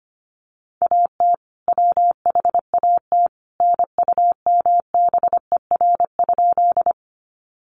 Morse code, heard as a keyed tone, about 25 wpm, 700 Hz. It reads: AT WHAT NUMBER?
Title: Sentences from Top 100 Words 25wpm